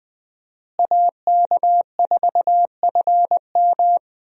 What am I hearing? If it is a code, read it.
Morse code, heard as a keyed tone, about 20 wpm, 700 Hz. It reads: AK4FM
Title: USA Call Signs - 5 Characters - Repeated 2x - 20wpm